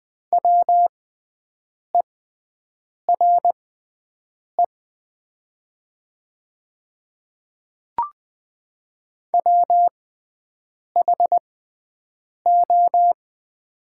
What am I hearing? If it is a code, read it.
Morse code, heard as a keyed tone, about 20 wpm, spaced as for 7 wpm, 700 Hz. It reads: WERE WHO
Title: 6x Character Spacing - Top 100 Words - 20wpm